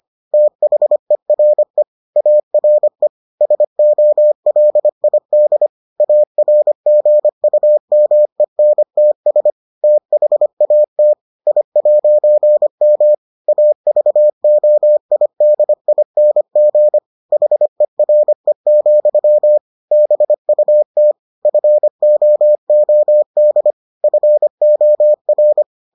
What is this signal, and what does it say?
Morse code, heard as a keyed tone, about 25 wpm, 600 Hz. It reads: THERE ARE SOLID ARGUMENTS THAT I'M AVOIDING HERE, BUT FOOD FOR